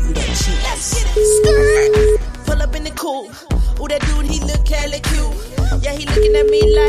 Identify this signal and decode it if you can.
Ringback tone (Europe, 425 Hz), with noise.